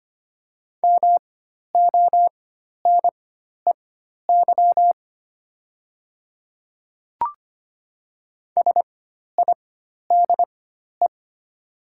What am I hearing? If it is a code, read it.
Morse code, heard as a keyed tone, about 25 wpm, spaced as for 12 wpm, 700 Hz. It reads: MONEY SIDE
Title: 4x Character Spacing - Top 500 Words - 25wpm